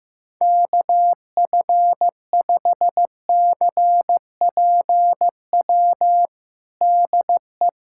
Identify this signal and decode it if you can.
Morse code, heard as a keyed tone, about 15 wpm, 700 Hz. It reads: KF5CPW DE
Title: USA Callsign DE Callsign - 6 Characters - 15wpm